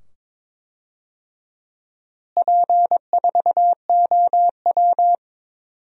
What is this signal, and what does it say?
Morse code, heard as a keyed tone, about 22 wpm, 700 Hz. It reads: P4OW
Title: DXCC-100 Call Signs - 4 Characters - Repeated 2x - 22wpm